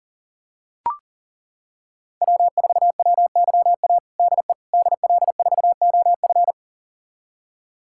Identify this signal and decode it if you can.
Morse code, heard as a keyed tone, about 40 wpm, 700 Hz. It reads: W4WYA DE DL4OF
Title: USA Callsign DE Callsign - 5 Characters - 40wpm